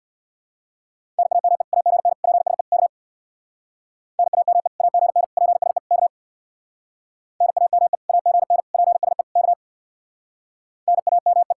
Morse code, heard as a keyed tone, about 65 wpm, 700 Hz. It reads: NAME AGN PSE K NAME AGN PSE K NAME AGN PSE K NAME